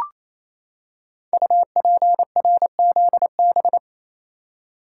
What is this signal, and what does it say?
Morse code, heard as a keyed tone, about 27 wpm, 700 Hz. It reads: UPRZB
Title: Code Groups - 5 Letters - 28wpm